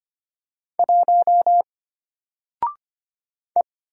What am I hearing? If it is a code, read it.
Morse code, heard as a keyed tone, about 25 wpm, 700 Hz. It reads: 1 E
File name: Single-Letter-Number-25wpm